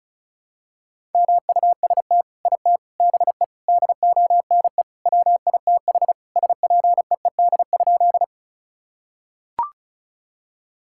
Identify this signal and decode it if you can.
Morse code, heard as a keyed tone, about 34 wpm, 700 Hz. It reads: MUST IT BE DONE WITH SPEED?
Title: Sentences from Top 1000 Words - Review Part 5 - 35wpm